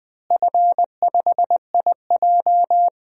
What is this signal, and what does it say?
Morse code, heard as a keyed tone, about 19 wpm, 700 Hz. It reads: F5IJ
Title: DXCC-100 Call Signs - 4 Characters - Repeated 2x - 20wpm